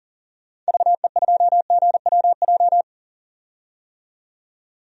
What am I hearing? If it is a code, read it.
Morse code, heard as a keyed tone, about 40 wpm, 700 Hz. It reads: VE2GWJ
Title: DXCC-100 Call Signs - 6 Characters - 40wpm